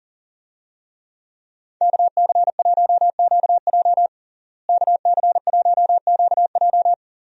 Morse code, heard as a keyed tone, about 40 wpm, 700 Hz. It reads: KC1QJ KC1QJ
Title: USA Call Signs - 5 Characters - Repeated 2x - 40wpm